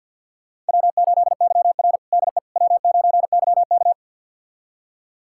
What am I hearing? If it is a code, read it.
Morse code, heard as a keyed tone, about 47 wpm, 700 Hz. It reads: W8YR DE W9YK
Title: USA Callsign DE Callsign - 4 Characters - 50wpm